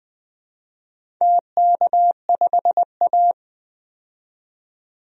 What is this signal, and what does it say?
Morse code, heard as a keyed tone, about 21 wpm, 700 Hz. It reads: TK5A
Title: DXCC-100 Call Signs - 4 Characters - 20wpm